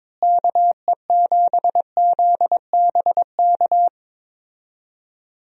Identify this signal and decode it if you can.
Morse code, heard as a keyed tone, about 21 wpm, 700 Hz. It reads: KE7ZBK
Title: USA Call Signs - 6 Characters - Repeated 3x - 22wpm